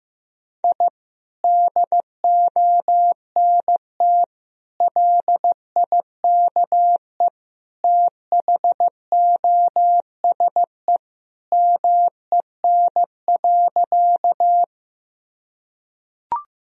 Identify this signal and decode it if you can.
Morse code, heard as a keyed tone, about 16 wpm, 700 Hz. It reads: I DONT LIKE THOSE MEN.
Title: Sentences from Top 1000 Words - Review Part 1 - 15wpm